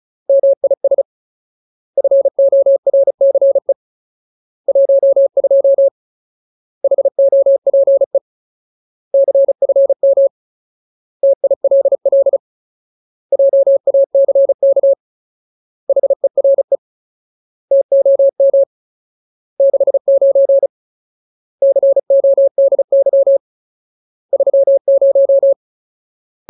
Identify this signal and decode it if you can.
Morse code, heard as a keyed tone, about 34 wpm, 550 Hz. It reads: MIS FORCE 12 HOPE CFM TILL JACK HERE TOM 69 CODY 30
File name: Ultimate Word List / 35WPM Ultimate